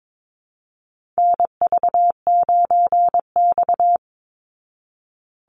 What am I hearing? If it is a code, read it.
Morse code, heard as a keyed tone, about 22 wpm, 700 Hz. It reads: NV9X
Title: Call Signs Repeated 2x - 22wpm